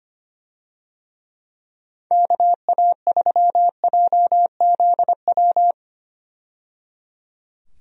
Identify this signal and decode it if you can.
Morse code, heard as a keyed tone, about 25 wpm, 700 Hz. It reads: KA3JZW
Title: USA Call Signs - 6 Characters - 25wpm